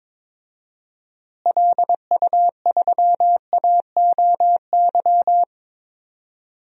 Morse code, heard as a keyed tone, about 22 wpm, 700 Hz. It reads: LU3AOY
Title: DXCC-100 Call Signs - 6 Characters - Repeated 3x - 22wpm